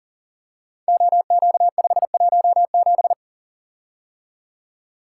Morse code, heard as a keyed tone, about 40 wpm, 700 Hz. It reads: OQ517
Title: Code Groups - 5 Letter:Numbers - 40wpm